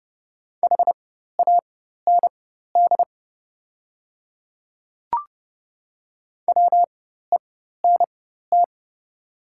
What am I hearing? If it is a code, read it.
Morse code, heard as a keyed tone, about 30 wpm, spaced as for 14 wpm, 700 Hz. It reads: HAND WENT
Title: 4x Character Spacing - Top 300 Words - 30wpm